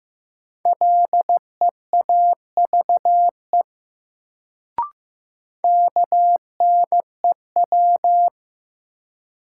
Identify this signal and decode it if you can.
Morse code, heard as a keyed tone, about 16 wpm, 700 Hz. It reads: LEAVE KNEW
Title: Sentences from Top 400 Words - New Word Drill - 15wpm